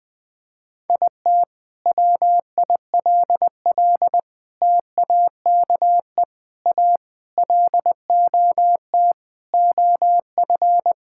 Morse code, heard as a keyed tone, about 20 wpm, 700 Hz. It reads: IT WILL TAKE A LOT OF
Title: Sentences from Top 1000 Words - Review Part 6 - 20wpm